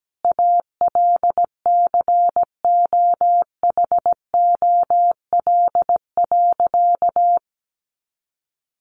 Morse code, heard as a keyed tone, about 17 wpm, 700 Hz. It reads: ALCOHOL.
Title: Sentences from Top 1400 Words 17wpm